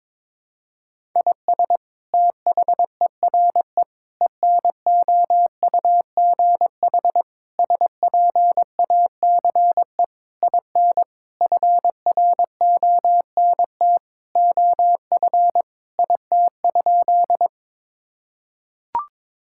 Morse code, heard as a keyed tone, about 22 wpm, 700 Hz. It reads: IS THERE ENOUGH SPACE IN FRONT OF IT?
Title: Sentences from Top 1000 Words - Review Part 5 - 22wpm